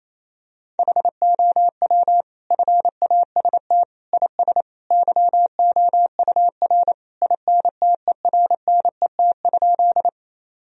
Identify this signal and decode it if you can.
Morse code, heard as a keyed tone, about 28 wpm, 700 Hz. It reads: HOW FAST IS YOUR INTERNET?
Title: Sentences from Top 1000 Words - Review Part 4 - 28wpm